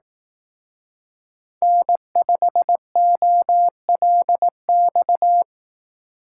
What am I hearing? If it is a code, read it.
Morse code, heard as a keyed tone, about 18 wpm, 700 Hz. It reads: N5OLX
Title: Speed Racing - USA 1x3 Call Signs 15wpm